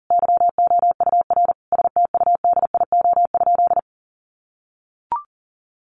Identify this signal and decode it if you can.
Morse code, heard as a keyed tone, about 40 wpm, 700 Hz. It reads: YOUR STUDIO?
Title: Sentences from Top 1200 Words 40wpm